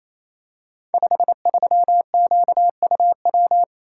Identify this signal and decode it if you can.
Morse code, heard as a keyed tone, about 29 wpm, 700 Hz. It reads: 53QUW
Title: Code Groups - 5 Letter:Numbers - 28wpm